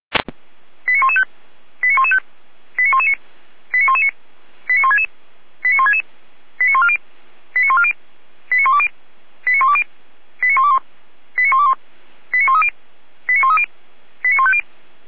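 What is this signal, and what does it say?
Five-tone selective call (ZVEI), heard as quick runs of five tones, tones of about 70 ms. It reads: 89115 89115 89119 89119 89166 89166 89140 89140 89120 89120 89121 89121 89130 89130 89150
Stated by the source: ZVEI five-tone selective call received by SDR.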